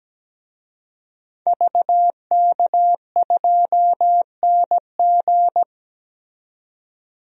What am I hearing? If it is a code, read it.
Morse code, heard as a keyed tone, about 17 wpm, 700 Hz. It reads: VK2NG
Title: DXCC-100 Call Signs - 5 Characters - 17wpm